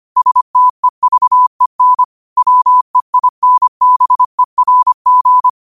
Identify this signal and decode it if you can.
Morse code, heard as a keyed tone, about 25 wpm, 1000 Hz. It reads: STEVEN WEINBERG